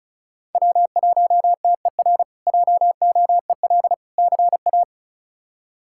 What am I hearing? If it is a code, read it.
Morse code, heard as a keyed tone, about 36 wpm, 700 Hz. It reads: W1TER JOEL CA
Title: K1USN SST QSO Practice - 35wpm